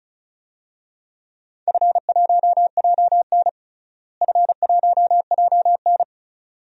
Morse code, heard as a keyed tone, about 36 wpm, 700 Hz. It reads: F1JN F1JN
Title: DXCC-100 Call Signs - 4 Characters - Repeated 2x - 35wpm